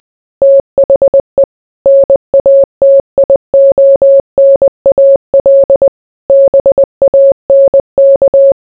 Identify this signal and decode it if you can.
Morse code, heard as a keyed tone, about 20 wpm, 550 Hz. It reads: THE NATIONAL BANK